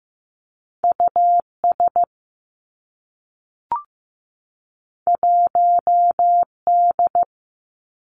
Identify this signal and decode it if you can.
Morse code, heard as a keyed tone, about 15 wpm, 700 Hz. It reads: U S 1 D
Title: Two Challenging Letter-Number Combinations 15wpm